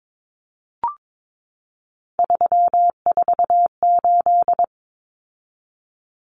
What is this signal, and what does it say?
Morse code, heard as a keyed tone, about 22 wpm, 700 Hz. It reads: 348
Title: Common Cut Numbers - 3-Digit 22wpm